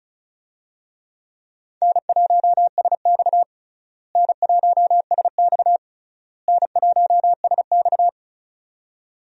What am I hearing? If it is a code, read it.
Morse code, heard as a keyed tone, about 34 wpm, 700 Hz. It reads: N1SX N1SX N1SX
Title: USA Call Signs - 4 Characters - Repeated 3x - 35wpm